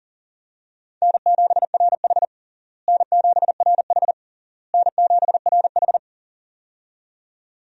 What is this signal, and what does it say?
Morse code, heard as a keyed tone, about 40 wpm, 700 Hz. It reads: N7RH N7RH N7RH
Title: USA Call Signs - 4 Characters - Repeated 3x - 40wpm